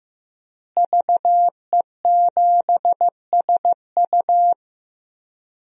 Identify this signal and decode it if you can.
Morse code, heard as a keyed tone, about 15 wpm, 700 Hz. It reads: VE7SU
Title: DXCC-100 Call Signs - 5 Characters - Repeated 2x - 15wpm